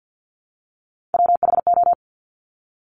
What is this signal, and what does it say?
Morse code, heard as a keyed tone, about 50 wpm, 700 Hz. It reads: WHO
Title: Top 200 Words Repeated 50wpm